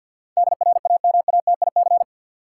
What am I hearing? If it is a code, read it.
Morse code, heard as a keyed tone, about 52 wpm, 700 Hz. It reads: DRAMATIC
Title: Sentences from Top 1300 Words - New Word Drill Alt - 50wpm